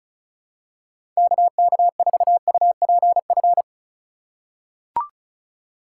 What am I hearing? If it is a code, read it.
Morse code, heard as a keyed tone, about 36 wpm, 700 Hz. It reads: KK4UPF
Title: USA Call Signs - 6 Characters - 3x Character Spacing - 35wpm